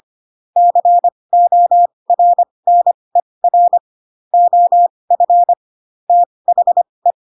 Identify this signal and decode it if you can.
Morse code, heard as a keyed tone, about 25 wpm, spaced as for 20 wpm, 700 Hz. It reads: CORNER OF THE
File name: Pinocchio / Pinocchio ch7 15 25 ik1hge